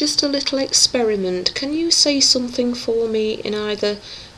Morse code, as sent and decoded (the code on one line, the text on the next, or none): none